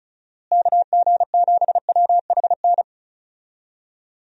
Morse code, heard as a keyed tone, about 36 wpm, 700 Hz. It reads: KG7WHN
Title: USA Call Signs - 6 Characters - Repeated 2x - 35wpm